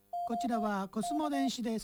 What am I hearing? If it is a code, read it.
Morse code, heard as a keyed tone, about 23 wpm, spaced as for 16 wpm, 700 Hz, with speech. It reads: OK